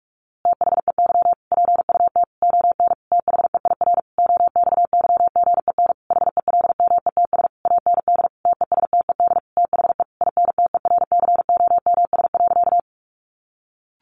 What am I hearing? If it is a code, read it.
Morse code, heard as a keyed tone, about 45 wpm, 700 Hz. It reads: THEY PUT ON THEIR OXYGEN HELMETS AND TESTED THE INTERCOMS.
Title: A Sound Of Thunder 45wpm